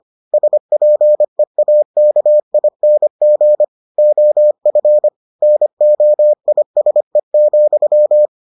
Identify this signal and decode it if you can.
Morse code, heard as a keyed tone, about 25 wpm, 600 Hz. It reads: SPEAKING OF NOISE,